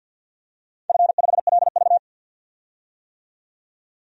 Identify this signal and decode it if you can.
Morse code, heard as a keyed tone, about 53 wpm, 700 Hz. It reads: F5LV
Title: DXCC-100 Call Signs - 4 Characters - 50wpm